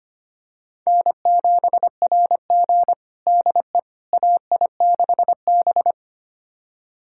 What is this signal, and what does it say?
Morse code, heard as a keyed tone, about 25 wpm, 700 Hz. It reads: N7RG DE AI6B
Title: USA Callsign DE Callsign - 4 Characters - 25wpm